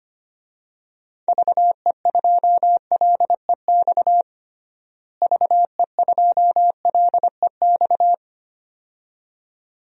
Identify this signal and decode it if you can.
Morse code, heard as a keyed tone, about 25 wpm, 700 Hz. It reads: VE2LEX VE2LEX
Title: DXCC-100 Call Signs - 6 Characters - Repeated 2x - 25wpm